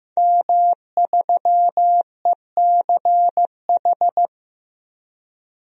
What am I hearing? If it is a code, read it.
Morse code, heard as a keyed tone, about 15 wpm, 700 Hz. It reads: M3ECH
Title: DXCC-100 Call Signs - 5 Characters - Repeated 2x - 15wpm